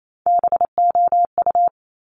Morse code, heard as a keyed tone, about 28 wpm, 700 Hz. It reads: BOU
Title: Top Three Letter Combinations in Words 28wpm